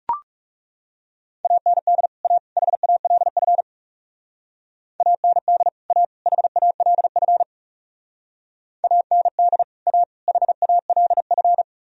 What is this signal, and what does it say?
Morse code, heard as a keyed tone, about 40 wpm, 700 Hz. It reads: AND A HALF AND A HALF AND A HALF